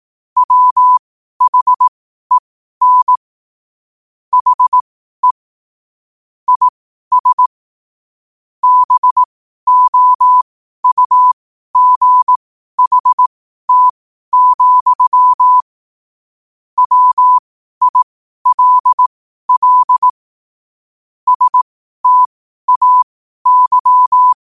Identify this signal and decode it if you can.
Morse code, heard as a keyed tone, about 18 wpm, spaced as for 12 wpm, 1000 Hz. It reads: WHEN HE IS BOUGHT, WILL STAY